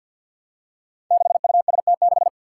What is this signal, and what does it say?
Morse code, heard as a keyed tone, about 51 wpm, 700 Hz. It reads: 6UST6
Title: Five Challenging Letter-Number Combinations 50wpm